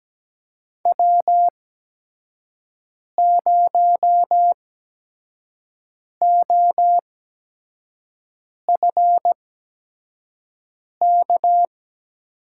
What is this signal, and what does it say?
Morse code, heard as a keyed tone, about 17 wpm, spaced as for 5 wpm, 700 Hz. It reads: W0OFK
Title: USA Call Signs - 5 Characters - 8x Character Spacing - 17wpm